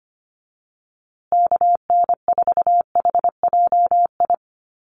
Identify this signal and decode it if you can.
Morse code, heard as a keyed tone, about 25 wpm, 700 Hz. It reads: KN4HJI
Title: USA Call Signs Repeated 2x 25wpm